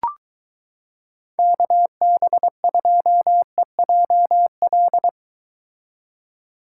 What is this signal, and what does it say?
Morse code, heard as a keyed tone, about 23 wpm, 700 Hz. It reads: KB2EJL